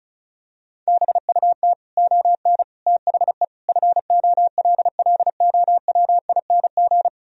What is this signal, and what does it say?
Morse code, heard as a keyed tone, about 36 wpm, 700 Hz. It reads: BUT ON THE FOLLOWING
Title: Sentences from Top 1000 Words - Review Part 4 - 35wpm